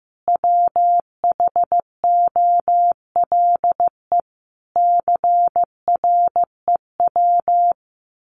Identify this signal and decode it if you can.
Morse code, heard as a keyed tone, about 15 wpm, 700 Hz. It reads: WHOLE CREW
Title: Sentences from Top 1500 Words 15wpm